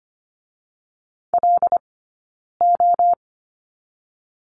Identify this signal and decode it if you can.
Morse code, heard as a keyed tone, about 25 wpm, 700 Hz. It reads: L O
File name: Lesson 15d - Number 5 - Two Characters 25wpm